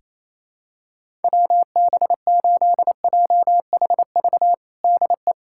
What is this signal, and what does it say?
Morse code, heard as a keyed tone, about 28 wpm, 700 Hz. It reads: WB8JHV DE
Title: USA Callsign DE Callsign 28wpm